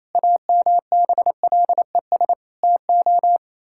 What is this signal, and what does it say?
Morse code, heard as a keyed tone, about 28 wpm, 700 Hz. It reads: AMBLES TO
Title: Pangram Challenge v2 - 28wpm